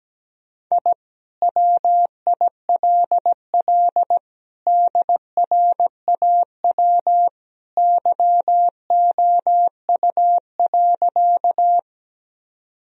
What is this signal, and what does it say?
Morse code, heard as a keyed tone, about 17 wpm, 700 Hz. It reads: I WILL DRAW YOU.
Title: Sentences from Top 1000 Words - Review Part 1 - 17wpm